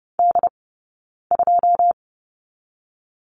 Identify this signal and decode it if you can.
Morse code, heard as a keyed tone, about 30 wpm, 700 Hz. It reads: D 2
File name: Lesson 15d - Number 5 - Two Characters 30wpm